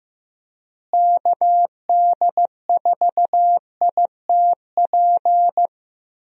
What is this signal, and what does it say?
Morse code, heard as a keyed tone, about 15 wpm, 700 Hz. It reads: KD4ITP
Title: USA Call Signs - 6 Characters - Repeated 2x - 15wpm